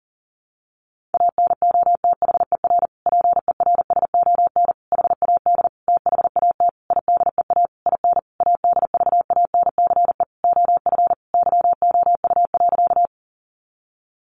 Morse code, heard as a keyed tone, about 40 wpm, 700 Hz. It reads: ANOTHER PERSON HAD THAT IDEA IN ADVANCE OF YOU.
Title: Sentences from Top 1400 Words 40wpm